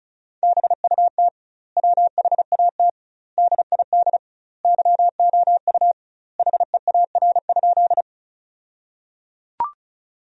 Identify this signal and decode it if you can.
Morse code, heard as a keyed tone, about 34 wpm, 700 Hz. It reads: BUT WHAT DID YOU HEAR?
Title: 2x Word-Spacing - Sentences from Top 300 Words - 35wpm